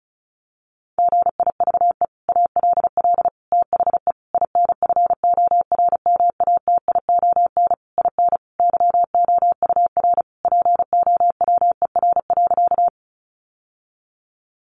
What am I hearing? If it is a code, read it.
Morse code, heard as a keyed tone, about 34 wpm, 700 Hz. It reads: GIVE ALL THE INFORMATION IN YOUR POWER.
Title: Sentences from Top 600 Words 35wpm